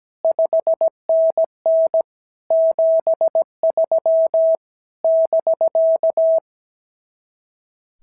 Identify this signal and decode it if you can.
Morse code, heard as a keyed tone, about 16 wpm, 650 Hz. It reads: 5NN 73 <BK>
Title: SOTA QSO Practice 17wpm